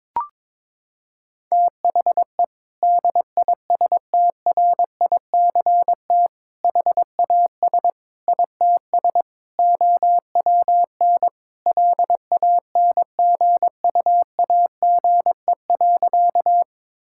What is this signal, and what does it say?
Morse code, heard as a keyed tone, about 22 wpm, 700 Hz. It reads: THE DISTRICT HAS ITS OWN LANGUAGE.
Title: Sentences from Top 1600 Words 22wpm